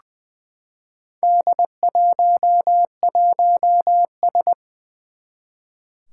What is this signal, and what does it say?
Morse code, heard as a keyed tone, about 20 wpm, 700 Hz. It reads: D11S
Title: Four Challenging Letter-Number Combinations 20wpm